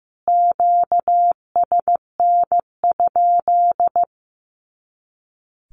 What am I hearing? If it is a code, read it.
Morse code, heard as a keyed tone, about 15 wpm, 700 Hz. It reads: QSN?
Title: Q-Codes Repeated 3x 15wpm